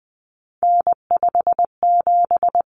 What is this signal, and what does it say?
Morse code, heard as a keyed tone, about 20 wpm, 700 Hz. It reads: N57
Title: Common Cut Numbers - 3-Digit 20wpm